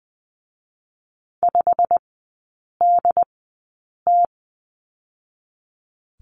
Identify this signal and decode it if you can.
Morse code, heard as a keyed tone, about 20 wpm, spaced as for 8 wpm, 700 Hz. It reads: 5DT
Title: 100 Lesson 17e - Letter u - Three Characters 20wpm